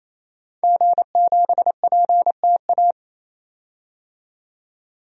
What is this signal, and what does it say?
Morse code, heard as a keyed tone, about 28 wpm, 700 Hz. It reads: G7PTA
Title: DXCC-100 Call Signs - 5 Characters - 28wpm